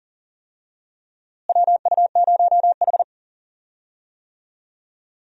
Morse code, heard as a keyed tone, about 37 wpm, 700 Hz. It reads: WU0H
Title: USA Call Signs - 4 Characters - Repeated 2x - 40wpm